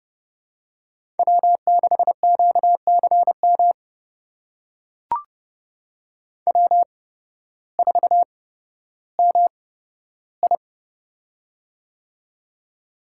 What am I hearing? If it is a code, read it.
Morse code, heard as a keyed tone, about 30 wpm, spaced as for 14 wpm, 700 Hz. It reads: W6QCM W4MI
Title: USA Call Signs - 8x Character Spacing - 30wpm